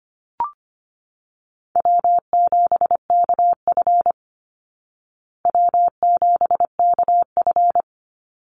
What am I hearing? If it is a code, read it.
Morse code, heard as a keyed tone, about 25 wpm, 700 Hz. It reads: W7KF W7KF
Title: Call Signs Repeated 3x - 25wpm